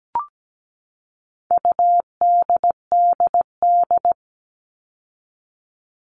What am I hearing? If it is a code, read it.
Morse code, heard as a keyed tone, about 17 wpm, 700 Hz. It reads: UDDD
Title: All Cut Numbers - 4-Digit 17wpm